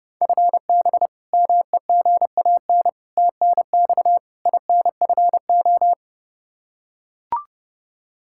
Morse code, heard as a keyed tone, about 30 wpm, 700 Hz. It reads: FB MEGAN TNX INFO
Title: Practice QSO - Exchange 3 - More Info - 30wpm